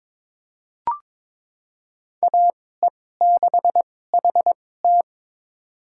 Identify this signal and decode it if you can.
Morse code, heard as a keyed tone, about 22 wpm, spaced as for 16 wpm, 700 Hz. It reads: AE6HT
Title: USA Call Signs - 2x Character Spacing - 22wpm